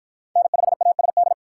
Morse code, heard as a keyed tone, about 46 wpm, 700 Hz. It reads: N5ASD